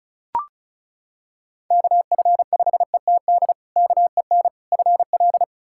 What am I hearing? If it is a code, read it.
Morse code, heard as a keyed tone, about 36 wpm, 700 Hz. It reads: KF5ETD KEN FL
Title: K1USN SST QSO Practice - 35wpm